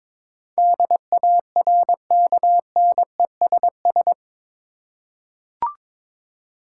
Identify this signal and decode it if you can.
Morse code, heard as a keyed tone, about 21 wpm, 700 Hz. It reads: DARKNESS
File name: practice / 22wpm / Eight Letter Words - 22wpm